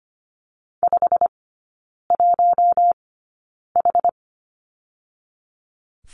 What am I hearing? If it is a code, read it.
Morse code, heard as a keyed tone, about 25 wpm, spaced as for 9 wpm, 690 Hz. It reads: 51H
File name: Lesson 15e - Number 5 - Three Characters 25wpm